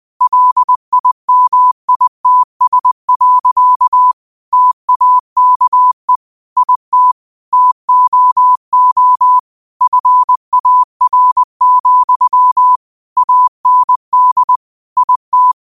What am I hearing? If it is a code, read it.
Morse code, heard as a keyed tone, about 20 wpm, 1000 Hz. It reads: LIMITS. TAKE IT TOO FAR, AND IT